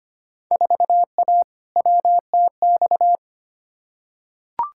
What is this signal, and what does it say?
Morse code, heard as a keyed tone, about 25 wpm, 700 Hz. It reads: 4A WTX
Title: Field Day Exchanges - 25wpm